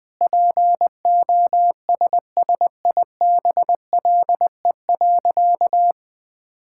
Morse code, heard as a keyed tone, about 20 wpm, 700 Hz. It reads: POSSIBLE.
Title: Sentences with 4-Word Phrases - 20wpm